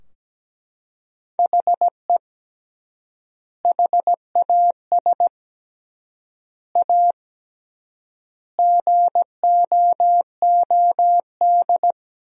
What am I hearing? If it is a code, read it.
Morse code, heard as a keyed tone, about 17 wpm, 700 Hz. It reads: HE HAS A GOOD
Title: 3x Word-Spacing - Sentences from Top 500 Words - 17wpm